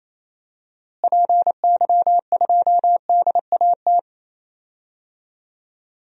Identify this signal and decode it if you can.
Morse code, heard as a keyed tone, about 28 wpm, 700 Hz. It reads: PY2DAT
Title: DXCC-100 Call Signs - 6 Characters - Repeated 3x - 28wpm